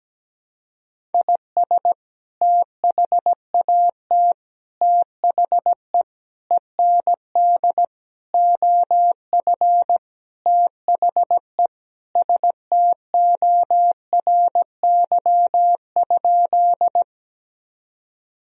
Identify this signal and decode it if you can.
Morse code, heard as a keyed tone, about 17 wpm, 700 Hz. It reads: IS THAT THE END OF THE STORY?
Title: Sentences from Top 1000 Words - Review Part 1 - 17wpm